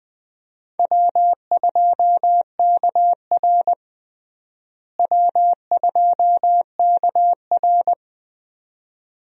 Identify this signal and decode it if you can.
Morse code, heard as a keyed tone, about 20 wpm, 700 Hz. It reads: W2KR W2KR
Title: USA Call Signs - 4 Characters - Repeated 2x - 20wpm